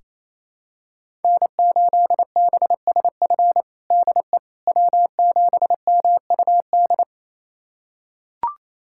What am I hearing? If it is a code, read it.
Morse code, heard as a keyed tone, about 27 wpm, 700 Hz. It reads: N8BSF DE W7MUD
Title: USA Callsign DE Callsign - 5 Characters - 28wpm